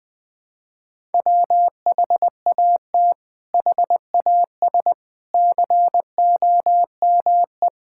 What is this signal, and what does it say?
Morse code, heard as a keyed tone, about 21 wpm, 700 Hz. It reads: WHAT HAS COME
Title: Sets of 8 Words - 20wpm